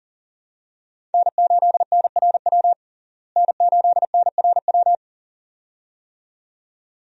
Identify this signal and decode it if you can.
Morse code, heard as a keyed tone, about 40 wpm, 700 Hz. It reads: N8NRW N8NRW
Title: USA Call Signs - 5 Characters - Repeated 2x - 40wpm